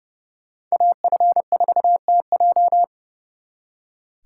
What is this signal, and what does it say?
Morse code, heard as a keyed tone, about 36 wpm, 700 Hz. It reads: AF4TJ
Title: Speed Racing - USA 2x2 Call Signs 30wpm